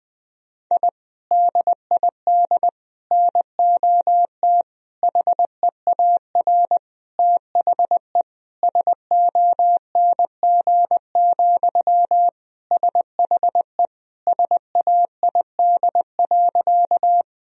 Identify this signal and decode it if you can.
Morse code, heard as a keyed tone, about 20 wpm, 700 Hz. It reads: I DID NOT HEAR THE SONG, SHE SAID.
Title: Sentences from Top 1000 Words - Review Part 2 - 20wpm